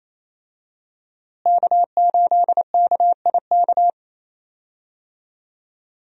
Morse code, heard as a keyed tone, about 28 wpm, 700 Hz. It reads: K8KIK
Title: USA Call Signs - 5 Characters - Repeated 2x - 28wpm